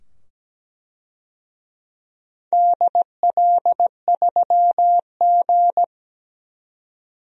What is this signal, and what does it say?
Morse code, heard as a keyed tone, about 17 wpm, 700 Hz. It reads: DL3G
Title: DXCC-100 Call Signs - 4 Characters - 17wpm